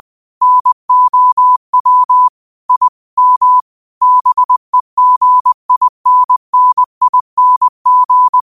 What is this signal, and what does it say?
Morse code, heard as a keyed tone, about 20 wpm, 1000 Hz. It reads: NOW I M BEGINNING
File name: qotdfiles / 20wpm